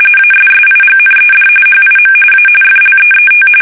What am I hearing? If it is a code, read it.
RTTY, about 45.45 baud. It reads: ASSIGNED HIM A FLEET
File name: fsk850